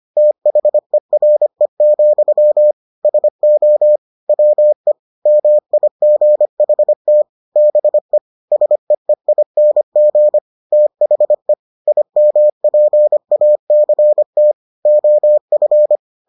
Morse code, heard as a keyed tone, about 25 wpm, 600 Hz. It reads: THERE, SO WE MIGHT BE SEEING THE IMPACT OF